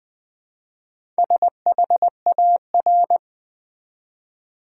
Morse code, heard as a keyed tone, about 20 wpm, 700 Hz. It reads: SHAR
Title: Build A Word - 1 Letter At A Time - 5-Letter Words - 20wpm